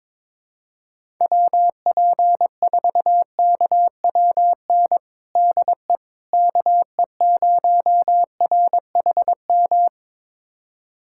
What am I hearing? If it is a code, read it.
Morse code, heard as a keyed tone, about 22 wpm, 700 Hz. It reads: WP4KWN DE KE0RHM
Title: USA Callsign DE Callsign - 6 Characters - 22wpm